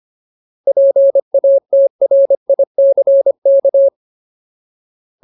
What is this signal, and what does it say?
Morse code, heard as a keyed tone, about 25 wpm, 550 Hz. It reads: PATRICK
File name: Ultimate Word List / 25WPM Ultimate